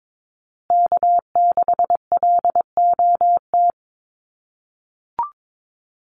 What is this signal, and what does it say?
Morse code, heard as a keyed tone, about 22 wpm, 700 Hz. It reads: K6LOT
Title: USA Call Signs Repeated 2x 22wpm